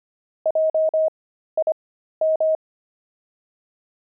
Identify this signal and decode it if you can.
Morse code, heard as a keyed tone, about 25 wpm, spaced as for 13 wpm, 630 Hz. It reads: JIM
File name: pangram-fs-13-cs-25-zebras